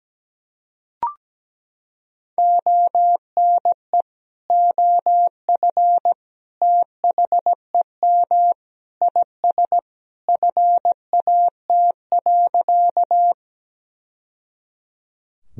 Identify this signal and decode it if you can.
Morse code, heard as a keyed tone, about 17 wpm, 700 Hz. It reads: ONE OF THEM IS FAT.
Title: Sentences from Top 1000 Words - Review Part 4 - 17wpm